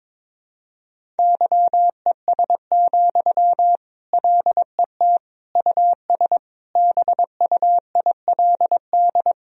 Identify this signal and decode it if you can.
Morse code, heard as a keyed tone, about 22 wpm, 700 Hz. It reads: YES, LET US BUILD
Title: Sentences from Top 1000 Words - Review Part 1 - 22wpm